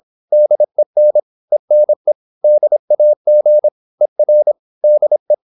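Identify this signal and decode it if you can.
Morse code, heard as a keyed tone, about 26 wpm, 600 Hz. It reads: DEN ENE DAG ER DE
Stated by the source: High-speed (QRQ) Morse at 26 wpm.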